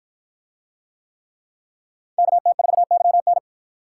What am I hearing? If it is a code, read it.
Morse code, heard as a keyed tone, about 53 wpm, 700 Hz. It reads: KT4YN